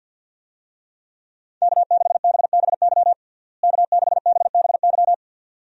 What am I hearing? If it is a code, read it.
Morse code, heard as a keyed tone, about 52 wpm, 700 Hz. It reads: K6BBY K6BBY
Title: USA Call Signs - 5 Characters - Repeated 2x - 50wpm